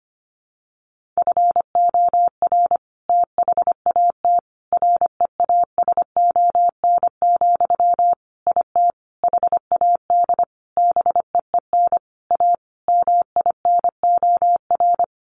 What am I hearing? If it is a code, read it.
Morse code, heard as a keyed tone, about 25 wpm, 700 Hz. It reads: FOR THAT REASON, IT HAD BEEN A MINOR
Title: Sentences from Top 1500 Words 25wpm